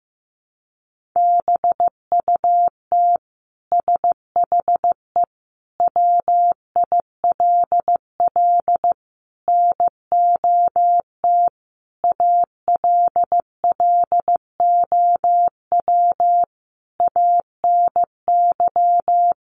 Morse code, heard as a keyed tone, about 15 wpm, 700 Hz. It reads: BUT SHE WILL NOT ALLOW ANY
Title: Sentences from Top 1200 Words 15wpm